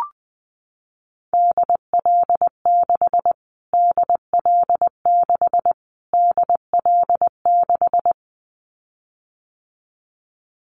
Morse code, heard as a keyed tone, about 20 wpm, 700 Hz. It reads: DL6 DL6 DL6
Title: Callsign Prefixes for Top 25 DXCC - Repeated 3x - 20wpm